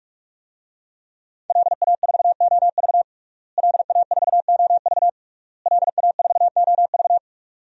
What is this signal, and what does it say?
Morse code, heard as a keyed tone, about 46 wpm, 700 Hz. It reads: LA4OV LA4OV LA4OV
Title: DXCC-100 Call Signs - 5 Characters - Repeated 3x - 45wpm